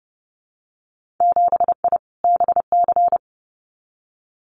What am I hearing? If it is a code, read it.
Morse code, heard as a keyed tone, about 30 wpm, 700 Hz. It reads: 7I BC
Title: Winter Field Day Exchanges Encore 30wpm